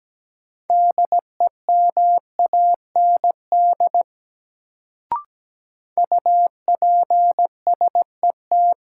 Morse code, heard as a keyed tone, about 17 wpm, 700 Hz. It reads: DEMAND UPSET
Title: Sentences from Top 800 Words - New Word Drill - 17wpm